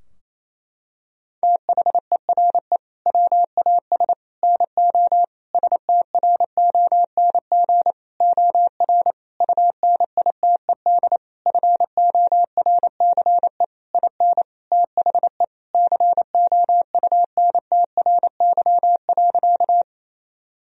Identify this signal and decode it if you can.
Morse code, heard as a keyed tone, about 27 wpm, 700 Hz. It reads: THERE WAS NO STRONG OR UNITED FORCE IN THE COUNTRY.
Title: Sentences from Top 1000 Words - Review Part 4 - 28wpm